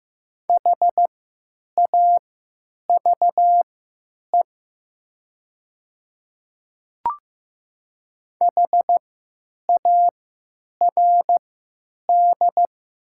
Morse code, heard as a keyed tone, about 15 wpm, spaced as for 9 wpm, 700 Hz. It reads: HAVE HARD
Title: 3x Character Spacing - Top 500 Words - 15wpm